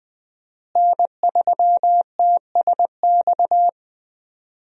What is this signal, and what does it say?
Morse code, heard as a keyed tone, about 20 wpm, 700 Hz. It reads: N3TSX
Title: USA Call Signs - 5 Characters - Repeated 2x - 20wpm